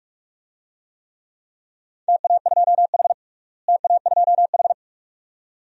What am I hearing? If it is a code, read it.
Morse code, heard as a keyed tone, about 46 wpm, 700 Hz. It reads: TA2H TA2H
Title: DXCC-100 Call Signs - 4 Characters - Repeated 2x - 45wpm